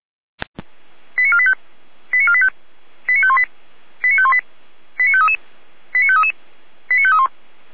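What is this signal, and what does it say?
Five-tone selective call (ZVEI), heard as quick runs of five tones, tones of about 80 ms. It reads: 89485 89485 89518 89518 89533 89533 89531
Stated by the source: ZVEI five-tone selective call received on an SDR.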